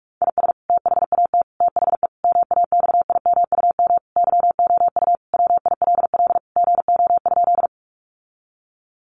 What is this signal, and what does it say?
Morse code, heard as a keyed tone, about 44 wpm, 700 Hz. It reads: IS THAT THE MAXIMUM YOU WILL GO?
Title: Sentences from Top 1300 Words 45wpm